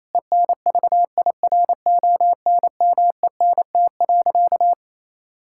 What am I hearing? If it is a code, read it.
Morse code, heard as a keyed tone, about 28 wpm, 700 Hz. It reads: ENVIRONMENT.
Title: Sentences from Top 1000 Words - Review Part 5 - 28wpm